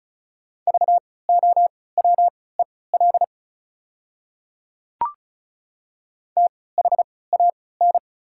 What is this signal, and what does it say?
Morse code, heard as a keyed tone, about 34 wpm, spaced as for 20 wpm, 700 Hz. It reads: VOWEL THAN
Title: 3x Character Spacing - Top 500 Words - 35wpm